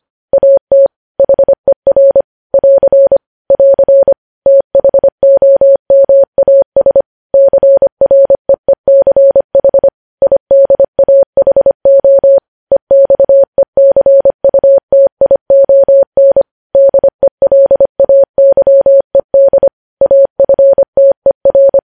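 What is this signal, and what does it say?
Morse code, heard as a keyed tone, about 25 wpm, 550 Hz. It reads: AT HER <AR> <AR> THOMAS CREECH IDAHO EXECUTION DELAYED AFTER
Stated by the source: News headlines sent in Morse at 25 wpm.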